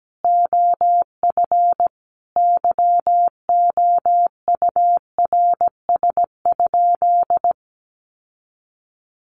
Morse code, heard as a keyed tone, about 17 wpm, 700 Hz. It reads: OF YOURS?
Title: Sentences from Top 1500 Words 17wpm